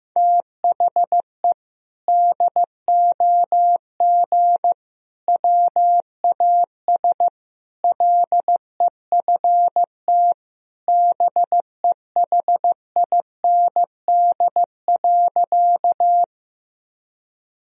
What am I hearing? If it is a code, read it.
Morse code, heard as a keyed tone, about 15 wpm, 700 Hz. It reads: THE DOG WAS LEFT BEHIND.
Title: Sentences from Top 1000 Words - Review Part 2 - 15wpm